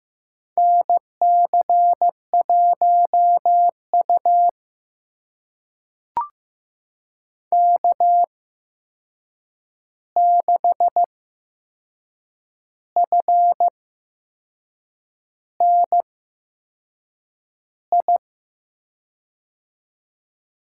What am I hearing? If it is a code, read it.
Morse code, heard as a keyed tone, about 15 wpm, spaced as for 6 wpm, 700 Hz. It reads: NC1U K6FNI
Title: USA Call Signs - 8x Character Spacing - 15wpm